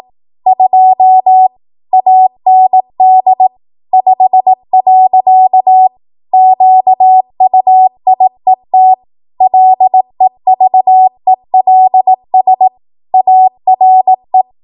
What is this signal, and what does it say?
Morse code, heard as a keyed tone, about 18 wpm, 750 Hz. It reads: 2 AND 5. QUIET LEVELS ARE
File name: Morse / 18WPM